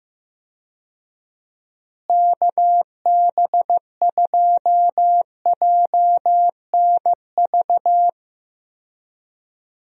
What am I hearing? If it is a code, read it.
Morse code, heard as a keyed tone, about 15 wpm, 700 Hz. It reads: KB2JNV
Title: USA Call Signs - 6 Characters - Repeated 3x - 15wpm